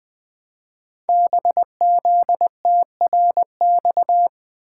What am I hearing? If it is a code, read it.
Morse code, heard as a keyed tone, about 20 wpm, 700 Hz. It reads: BZTRX
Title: Code Groups - 5 Letter:Numbers - 20wpm